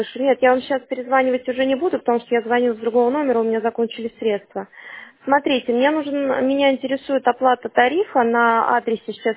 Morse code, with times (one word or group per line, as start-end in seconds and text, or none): none